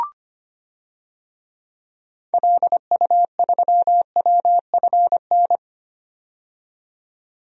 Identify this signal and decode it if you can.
Morse code, heard as a keyed tone, about 25 wpm, 700 Hz. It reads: LU3WFN
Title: DXCC-100 Call Signs - 6 Characters - 25wpm